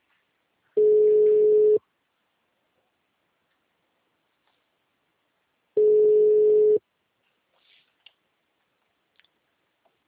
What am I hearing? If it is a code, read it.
Ringback tone (Europe, 425 Hz).